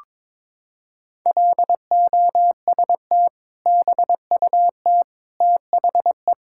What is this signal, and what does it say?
Morse code, heard as a keyed tone, about 22 wpm, 700 Hz. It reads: LOST BUT THE
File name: practice / 22wpm / Sentences with 2-Word Phrases - 22wpm